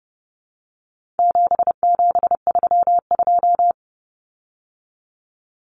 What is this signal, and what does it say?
Morse code, heard as a keyed tone, about 30 wpm, 700 Hz. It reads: 7732
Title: Common Cut Numbers - 4-Digit 30wpm